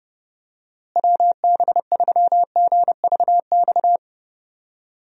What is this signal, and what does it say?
Morse code, heard as a keyed tone, about 32 wpm, 700 Hz. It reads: WB3GVX
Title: USA Call Signs - 6 Characters - Repeated 2x - 30wpm